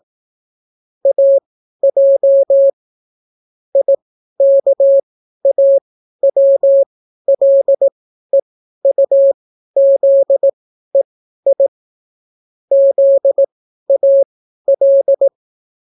Morse code, heard as a keyed tone, about 18 wpm, spaced as for 12 wpm, 550 Hz. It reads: AJ IKAWLEUZEI ZAL